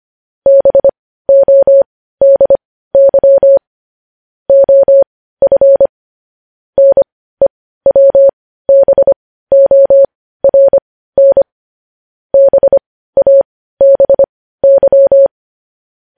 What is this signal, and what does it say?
Morse code, heard as a keyed tone, about 25 wpm, spaced as for 15 wpm, 550 Hz. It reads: BODY OF NEWBORN BABY
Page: News Headlines in Morse Code at 15 WPM